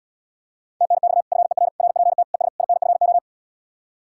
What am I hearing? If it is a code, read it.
Morse code, heard as a keyed tone, about 77 wpm, 700 Hz. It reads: TNX FER RPT ES INFO